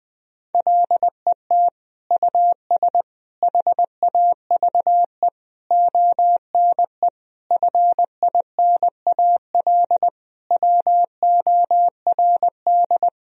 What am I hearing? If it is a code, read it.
Morse code, heard as a keyed tone, about 21 wpm, 700 Hz. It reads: LET US HAVE ONE FINAL WORD
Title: Sentences from Top 1000 Words - Review Part 2 - 20wpm